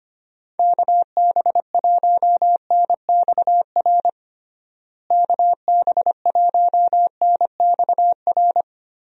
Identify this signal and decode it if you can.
Morse code, heard as a keyed tone, about 25 wpm, 700 Hz. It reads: KB1NXR KB1NXR
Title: USA Call Signs - 6 Characters - Repeated 3x - 25wpm